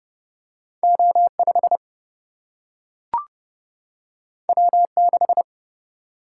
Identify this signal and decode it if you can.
Morse code, heard as a keyed tone, about 30 wpm, 700 Hz. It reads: O5 W6
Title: Single-Letter Single-Number 30wpm